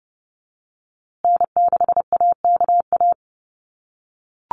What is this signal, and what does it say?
Morse code, heard as a keyed tone, about 30 wpm, 700 Hz. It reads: N6AKA
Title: USA Call Signs Repeated 2x 30wpm